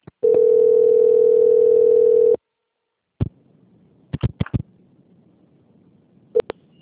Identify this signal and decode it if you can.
Ringback tone (North America), 440 + 480 Hz.